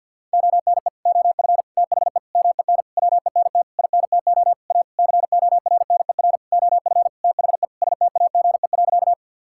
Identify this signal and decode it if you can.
Morse code, heard as a keyed tone, about 50 wpm, 700 Hz. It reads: ONE OF THE MEN WENT INTO A CORNER OF THE STAGE.
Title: Sentences from Top 1600 Words 50wpm